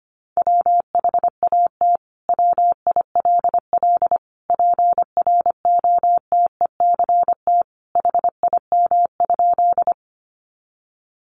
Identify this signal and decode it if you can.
Morse code, heard as a keyed tone, about 25 wpm, 700 Hz. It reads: WHAT WILL PROTECT HIM?
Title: Sentences from Top 900 Words 25wpm